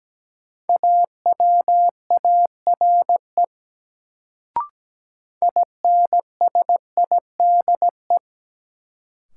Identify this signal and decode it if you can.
Morse code, heard as a keyed tone, about 17 wpm, 700 Hz. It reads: AWARE INSIDE
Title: Sentences from Top 1200 Words - New Word Drill Alt - 17wpm